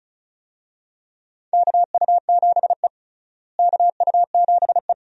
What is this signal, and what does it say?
Morse code, heard as a keyed tone, about 34 wpm, 700 Hz. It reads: KU7E KU7E
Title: USA Call Signs - 4 Characters - Repeated 2x - 35wpm